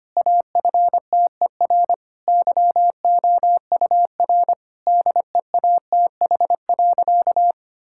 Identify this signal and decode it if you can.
Morse code, heard as a keyed tone, about 25 wpm, 700 Hz. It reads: AFTER YOUR DEATH.
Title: Sentences from Top 1600 Words 25wpm